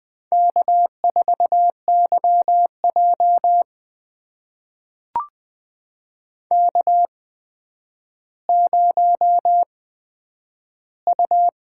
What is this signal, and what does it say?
Morse code, heard as a keyed tone, about 20 wpm, spaced as for 10 wpm, 700 Hz. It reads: K4YJ K0U
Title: USA Call Signs - 4 Characters - 8x Character Spacing - 20wpm